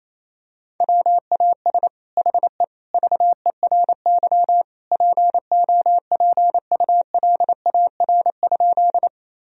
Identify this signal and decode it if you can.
Morse code, heard as a keyed tone, about 28 wpm, 700 Hz. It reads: WAS HE VERY POPULAR?
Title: Sentences from Top 1000 Words - Review Part 3 - 28wpm